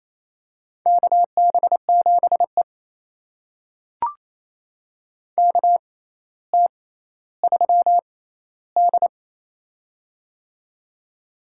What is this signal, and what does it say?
Morse code, heard as a keyed tone, about 28 wpm, spaced as for 14 wpm, 700 Hz. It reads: KB7E KT3D
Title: USA Call Signs - 6x Character Spacing - 28wpm